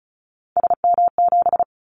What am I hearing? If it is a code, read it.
Morse code, heard as a keyed tone, about 36 wpm, 700 Hz. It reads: SM7
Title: Callsign Prefixes for Top 25 DXCC - 35wpm